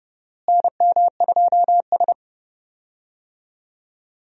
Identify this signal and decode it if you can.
Morse code, heard as a keyed tone, about 30 wpm, 700 Hz. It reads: NM2S
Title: USA Call Signs - 4 Characters - Repeated 3x - 30wpm